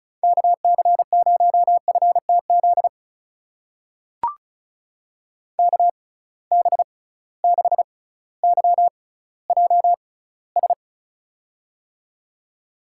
Morse code, heard as a keyed tone, about 36 wpm, spaced as for 18 wpm, 690 Hz. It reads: KC0FTZ KB6YJS
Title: USA Call Signs - 6 Characters - 6x Character Spacing - 35wpm